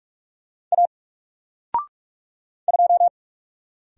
Morse code, heard as a keyed tone, about 44 wpm, 700 Hz. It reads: A 2
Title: Single Character (Letter:Number:Punctuation:Prosign) 45wpm